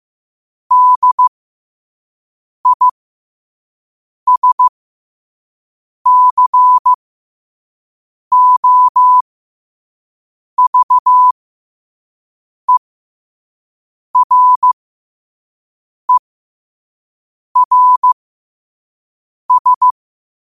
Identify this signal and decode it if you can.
Morse code, heard as a keyed tone, about 15 wpm, spaced as for 5 wpm, 1000 Hz. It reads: DISCOVERERS
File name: qotdfiles / 5wpm